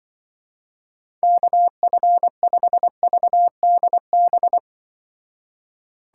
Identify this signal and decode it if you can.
Morse code, heard as a keyed tone, about 25 wpm, 700 Hz. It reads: KF5VDB
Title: Speed Racing - USA Call Signs 20wpm